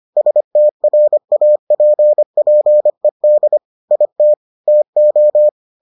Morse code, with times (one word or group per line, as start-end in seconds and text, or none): 0.2-3.6 STRAPPED
3.9-4.3 IT
4.7-5.5 TO